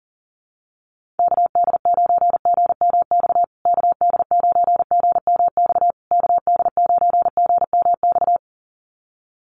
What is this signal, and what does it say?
Morse code, heard as a keyed tone, about 40 wpm, 700 Hz. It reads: KD9GMX KD9GMX KD9GMX
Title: USA Call Signs Repeated 3x 40wpm